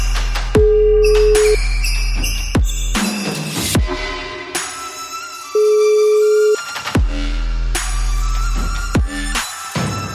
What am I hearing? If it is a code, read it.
Ringback tone (Europe, 425 Hz).